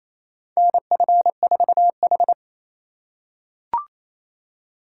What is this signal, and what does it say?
Morse code, heard as a keyed tone, about 27 wpm, 700 Hz. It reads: NF4H
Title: USA Call Signs - 4 Characters - 4x Character Spacing - 28wpm